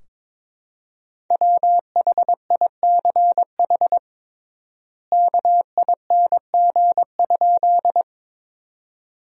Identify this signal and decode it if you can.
Morse code, heard as a keyed tone, about 22 wpm, 700 Hz. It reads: WHICH KING?
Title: 3x Word-Spacing - Sentences from Top 400 Words - 22wpm